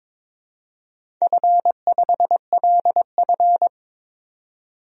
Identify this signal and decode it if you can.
Morse code, heard as a keyed tone, about 22 wpm, 700 Hz. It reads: F5LF
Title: DXCC-100 Call Signs - 4 Characters - 22wpm